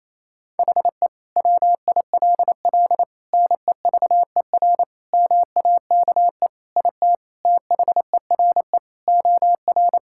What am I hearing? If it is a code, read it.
Morse code, heard as a keyed tone, about 28 wpm, 700 Hz. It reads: HE WILL NEVER MAKE IT THERE OR
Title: Sentences from Top 1600 Words 28wpm